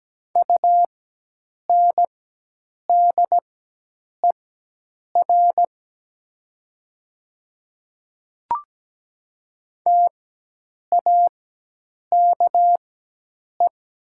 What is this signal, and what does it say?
Morse code, heard as a keyed tone, about 17 wpm, spaced as for 8 wpm, 700 Hz. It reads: UNDER TAKE
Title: 4x Character Spacing - Top 200 Words - 17wpm